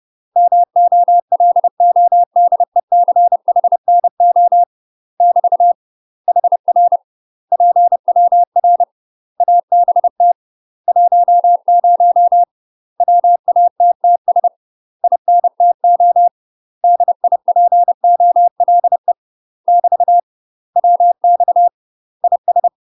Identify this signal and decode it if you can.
Morse code, heard as a keyed tone, about 30 wpm, 700 Hz. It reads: MOLODECHNO <BT> HR PWR ABT 10 WATTS INTO DIPOLE <BT> WX IS
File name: UK MASTER QSOs TRAINING/30WPM